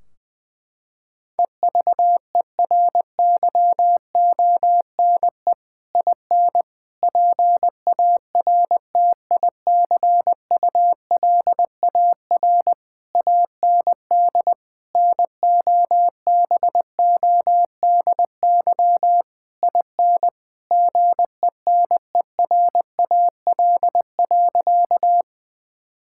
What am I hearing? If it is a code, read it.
Morse code, heard as a keyed tone, about 20 wpm, 700 Hz. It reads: EVERYONE IN PARTICULAR AND NOBODY IN GENERAL.
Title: Sentences from Top 1600 Words 20wpm